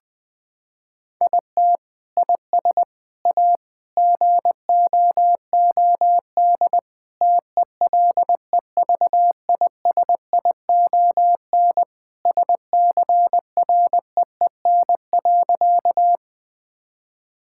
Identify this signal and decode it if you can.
Morse code, heard as a keyed tone, about 20 wpm, 700 Hz. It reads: IT IS A GOOD TELEVISION SCREEN.
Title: Sentences from Top 1000 Words - Review Part 6 - 20wpm